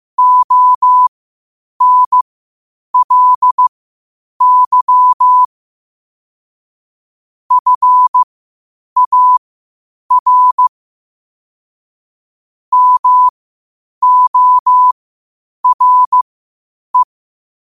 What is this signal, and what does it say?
Morse code, heard as a keyed tone, about 15 wpm, spaced as for 8 wpm, 1000 Hz. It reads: ONLY FAR MORE
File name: qotdfiles / 8wpm